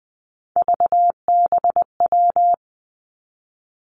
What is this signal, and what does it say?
Morse code, heard as a keyed tone, about 20 wpm, 700 Hz. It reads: VBW
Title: Three Challenging Letter-Number Combinations 20wpm